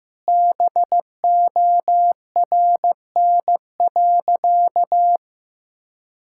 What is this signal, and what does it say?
Morse code, heard as a keyed tone, about 16 wpm, 700 Hz. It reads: BORN.
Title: Sentences from Top 1600 Words 15wpm